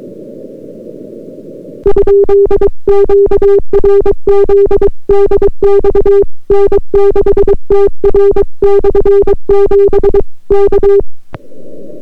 Morse code, heard as a keyed tone, about 22 wpm, 400 Hz. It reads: ?QRZ DX N6TR/7 K